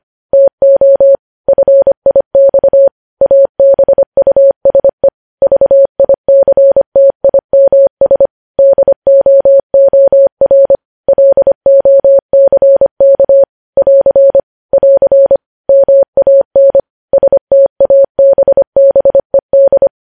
Morse code, heard as a keyed tone, about 25 wpm, 550 Hz. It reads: TO FIX ABUSE VICTIMS DOOR LOCK <AR> <AR> MAN STABBED
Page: news headlines in morse code at 25 wpm